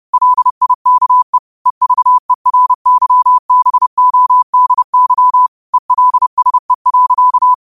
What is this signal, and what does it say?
Morse code, heard as a keyed tone, about 30 wpm, 1000 Hz. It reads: LIKE EVERYBODY ELSE.
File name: qotdfiles / 30wpm